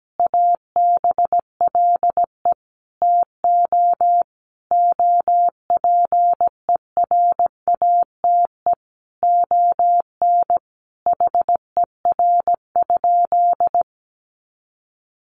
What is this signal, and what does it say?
Morse code, heard as a keyed tone, about 17 wpm, 700 Hz. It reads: ABLE TO OPERATE ON HER?
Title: Sentences from Top 1500 Words 17wpm